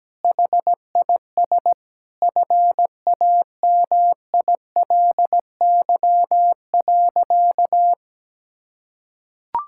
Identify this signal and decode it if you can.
Morse code, heard as a keyed tone, about 17 wpm, 700 Hz. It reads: HIS FAMILY.
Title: Sentences from Top 1000 Words - Review Part 6 - 17wpm